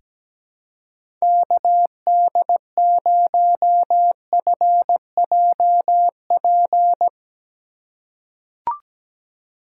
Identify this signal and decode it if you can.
Morse code, heard as a keyed tone, about 17 wpm, 700 Hz. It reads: KD0FJP
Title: USA Call Signs - 6 Characters - 2x Character Spacing - 17wpm